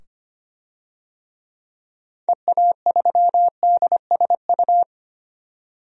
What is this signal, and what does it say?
Morse code, heard as a keyed tone, about 25 wpm, 700 Hz. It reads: EA3DSU
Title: DXCC-100 Call Signs - 6 Characters - 25wpm